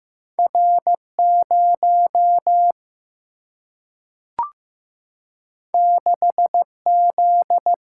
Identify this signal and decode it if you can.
Morse code, heard as a keyed tone, about 15 wpm, 700 Hz. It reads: R 0 6 Z
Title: Single-Letter Single-Number 15wpm